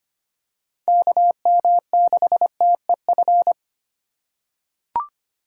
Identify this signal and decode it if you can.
Morse code, heard as a keyed tone, about 25 wpm, 700 Hz. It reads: KM6TEF
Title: USA Call Signs - 6 Characters - 3x Character Spacing - 25wpm